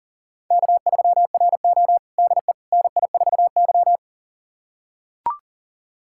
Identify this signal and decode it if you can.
Morse code, heard as a keyed tone, about 37 wpm, 700 Hz. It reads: K3RO DE NI4Y
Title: USA Callsign DE Callsign - 4 Characters - 40wpm